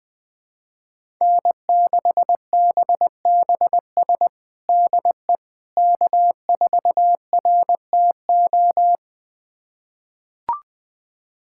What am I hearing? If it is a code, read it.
Morse code, heard as a keyed tone, about 20 wpm, 700 Hz. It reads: N6BBS DE K4RTO
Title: USA Callsign DE Callsign - 5 Characters - 20wpm